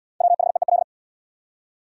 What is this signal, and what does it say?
Morse code, heard as a keyed tone, about 70 wpm, 700 Hz. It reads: WHEEL